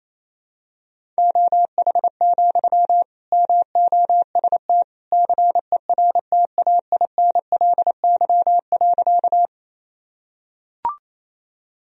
Morse code, heard as a keyed tone, about 28 wpm, 700 Hz. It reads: OH, MOST CERTAINLY.
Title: Sentences from Top 1000 Words - Review Part 1 - 28wpm